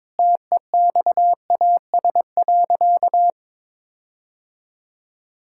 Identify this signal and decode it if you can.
Morse code, heard as a keyed tone, about 21 wpm, 700 Hz. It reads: TEXAS.
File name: practice / 22wpm / Sentences with 4-Word Phrases - 22wpm